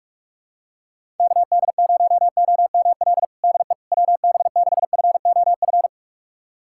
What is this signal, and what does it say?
Morse code, heard as a keyed tone, about 44 wpm, 700 Hz. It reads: KD0OML DE WB6FOF
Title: USA Callsign DE Callsign 45wpm